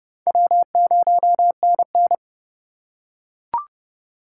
Morse code, heard as a keyed tone, about 29 wpm, 700 Hz. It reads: W0NN